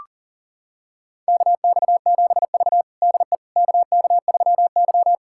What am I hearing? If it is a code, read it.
Morse code, heard as a keyed tone, about 40 wpm, 700 Hz. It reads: KX7V DE KK3Y
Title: USA Callsign DE Callsign - 4 Characters - 40wpm